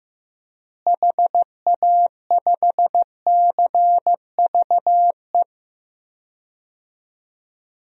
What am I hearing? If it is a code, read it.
Morse code, heard as a keyed tone, about 15 wpm, 700 Hz. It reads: HA5CVE
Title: DXCC-100 Call Signs - 6 Characters - Repeated 2x - 15wpm